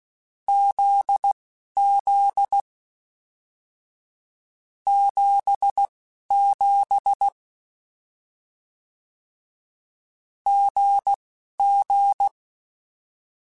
Morse code, heard as a keyed tone, about 16 wpm, spaced as for 11 wpm, 800 Hz. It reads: ZZ 77 GG
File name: Short Runs / Short Runs-mp3s / GZ GZ7 @10